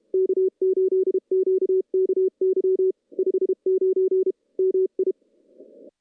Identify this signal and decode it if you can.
Morse code, heard as a keyed tone, about 32 wpm, 380 Hz. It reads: K8QKY 59 MI
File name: contests / CW-SS-k5tr- -0000